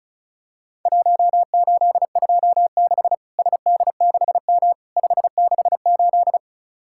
Morse code, heard as a keyed tone, about 36 wpm, 700 Hz. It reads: 1826 SD6M 568
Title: Worked All Europe DX Contest - QTC Exchange - 35wpm